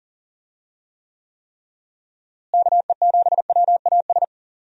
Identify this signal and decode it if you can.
Morse code, heard as a keyed tone, about 40 wpm, 700 Hz. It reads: KE7WAS
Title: USA Call Signs - 6 Characters - Repeated 3x - 40wpm